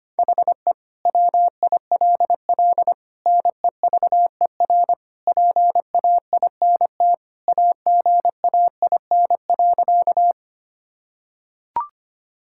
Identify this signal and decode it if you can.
Morse code, heard as a keyed tone, about 25 wpm, 700 Hz. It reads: HE WILL NEVER PAINT AGAIN.
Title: Sentences from Top 1000 Words - Review Part 6 - 25wpm